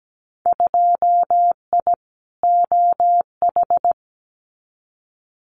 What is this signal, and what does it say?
Morse code, heard as a keyed tone, about 17 wpm, 700 Hz. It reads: 2I OH
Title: Winter Field Day Exchanges 17wpm